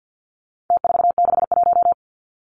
Morse code, heard as a keyed tone, about 49 wpm, 700 Hz. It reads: T461
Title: Numbers - Contest Style - 4-Digit 50wpm